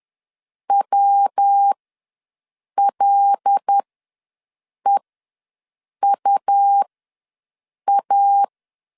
Morse code, heard as a keyed tone, about 10 wpm, spaced as for 6 wpm, 800 Hz. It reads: WLEUA
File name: LEZIONI / 5 LBVC / 5 L ws 10wpm 3636